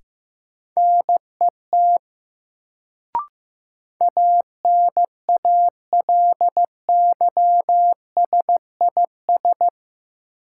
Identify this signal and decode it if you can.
Morse code, heard as a keyed tone, about 16 wpm, 700 Hz. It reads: NET ANALYSIS
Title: Sentences from Top 1400 Words - New Word Drill Alt - 15wpm